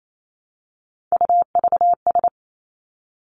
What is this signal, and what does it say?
Morse code, heard as a keyed tone, about 28 wpm, 700 Hz. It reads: UVS
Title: Three Challenging Letter-Number Combinations 28wpm